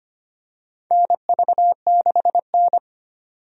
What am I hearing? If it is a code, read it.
Morse code, heard as a keyed tone, about 25 wpm, 700 Hz. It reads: NV6N
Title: USA Call Signs - 4 Characters - 25wpm